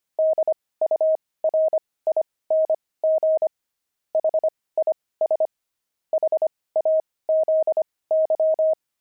Morse code, heard as a keyed tone, about 25 wpm, spaced as for 18 wpm, 630 Hz. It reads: DURING HIS HAZY
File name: pangram-fs-18-cs-25-weaving